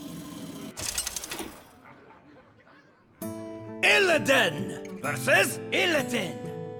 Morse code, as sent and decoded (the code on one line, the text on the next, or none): none